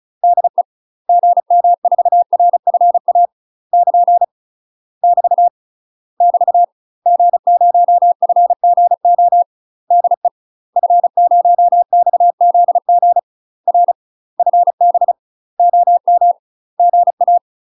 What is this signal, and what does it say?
Morse code, heard as a keyed tone, about 35 wpm, 700 Hz. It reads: DE GM4RFA <KN> <BT> <BT> G0FGO DE F0XZG R FB OM GA
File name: UK MASTER QSOs TRAINING/35WPM